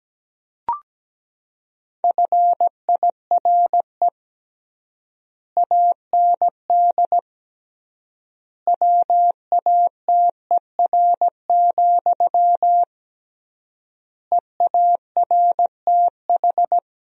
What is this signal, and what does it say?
Morse code, heard as a keyed tone, about 17 wpm, 700 Hz. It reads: FIRE AND WATER, EARTH
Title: 3x Word-Spacing - Sentences from Top 400 Words - 17wpm